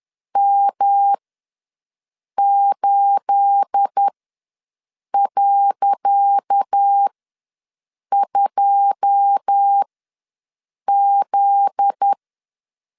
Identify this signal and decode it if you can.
Morse code, heard as a keyed tone, about 10 wpm, spaced as for 6 wpm, 800 Hz. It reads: M8.2Z
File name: LEZIONI / 10 SIGNS / 10C DOT ws 10wpm 4213